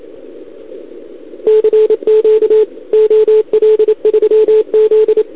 Morse code, heard as a keyed tone, about 27 wpm, 440 Hz. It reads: CQ OL3Z
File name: CQWWCW2021 / OL3Z